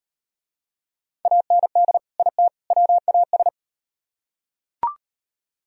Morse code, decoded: AND IT WAS

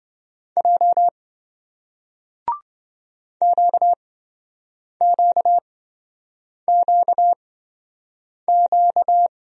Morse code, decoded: J QQQQ

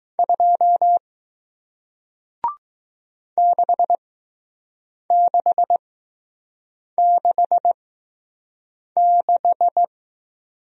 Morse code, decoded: 2 6666

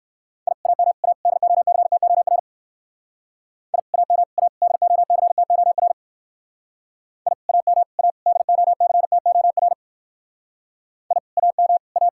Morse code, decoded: I AM A DOCTOR I AM A DOCTOR I AM A DOCTOR I AM A

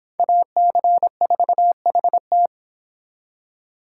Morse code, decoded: AC4HT

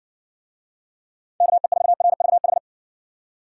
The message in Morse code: KE4RVH

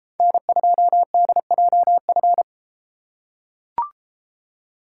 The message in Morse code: N2DJF

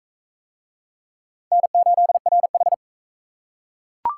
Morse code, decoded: N8RH

